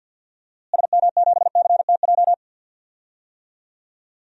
Morse code, heard as a keyed tone, about 52 wpm, 700 Hz. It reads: SM7CTJ
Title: DXCC-100 Call Signs - 6 Characters - Repeated 2x - 50wpm